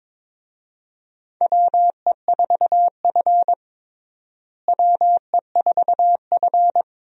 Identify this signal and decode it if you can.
Morse code, heard as a keyed tone, about 22 wpm, 700 Hz. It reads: WE4F WE4F
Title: USA Call Signs - 4 Characters - Repeated 3x - 22wpm